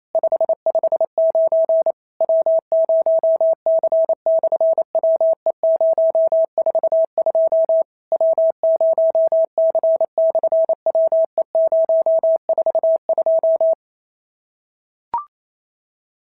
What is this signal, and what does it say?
Morse code, heard as a keyed tone, about 28 wpm, 690 Hz. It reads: 559 W0C/WE042 W0C/WE042
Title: SOTA QSO Practice 28wpm